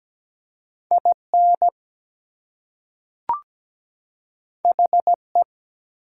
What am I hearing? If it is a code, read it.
Morse code, heard as a keyed tone, about 17 wpm, 700 Hz. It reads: IN HE
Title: Two Letter Words - 17wpm